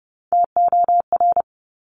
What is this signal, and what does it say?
Morse code, heard as a keyed tone, about 30 wpm, 700 Hz. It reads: TOR
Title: Top Three Letter Combinations in Words 30wpm